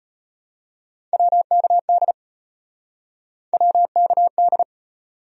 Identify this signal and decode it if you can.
Morse code, heard as a keyed tone, about 34 wpm, 700 Hz. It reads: WKD WKD